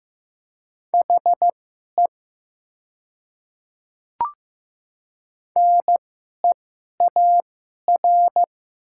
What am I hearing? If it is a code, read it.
Morse code, heard as a keyed tone, about 15 wpm, spaced as for 11 wpm, 700 Hz. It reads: HE NEAR
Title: 2x Character Spacing - Top 300 Words - 15wpm